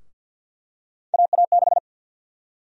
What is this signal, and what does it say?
Morse code, heard as a keyed tone, about 53 wpm, 700 Hz. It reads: AA6
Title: USA Call Sign Prefixes - Repeated 2x - 50wpm